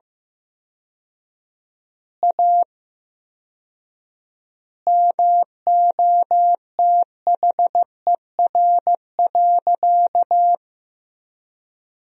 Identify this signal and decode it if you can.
Morse code, heard as a keyed tone, about 15 wpm, 700 Hz. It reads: A MOTHER.
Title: 4x Word-Spacing - Sentences from Top 200 Words - 15wpm